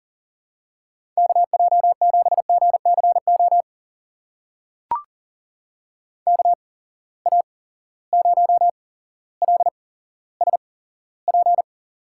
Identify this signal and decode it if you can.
Morse code, heard as a keyed tone, about 40 wpm, spaced as for 17 wpm, 700 Hz. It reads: KJ7GCO KA0LSP
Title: USA Call Signs - 6 Characters - 8x Character Spacing - 40wpm